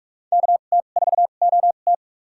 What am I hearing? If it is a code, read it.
Morse code, heard as a keyed tone, about 46 wpm, spaced as for 32 wpm, 700 Hz. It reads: KT4OT
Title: USA Call Signs - 5 Characters - 2x Character Spacing - 45wpm